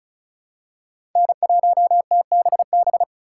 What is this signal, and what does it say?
Morse code, heard as a keyed tone, about 36 wpm, 700 Hz. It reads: N1TBB
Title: USA Call Signs - 4x Character Spacing - 35wpm